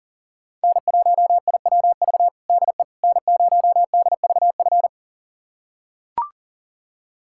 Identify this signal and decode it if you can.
Morse code, heard as a keyed tone, about 40 wpm, 700 Hz. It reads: N1IWV DE N0DVF
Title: USA Callsign DE Callsign - 5 Characters - 40wpm